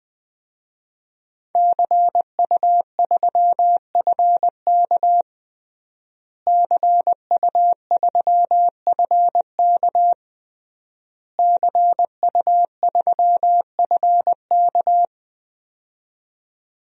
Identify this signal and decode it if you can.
Morse code, heard as a keyed tone, about 21 wpm, 700 Hz. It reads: CU3FK CU3FK CU3FK
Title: DXCC-100 Call Signs - 5 Characters - Repeated 3x - 20wpm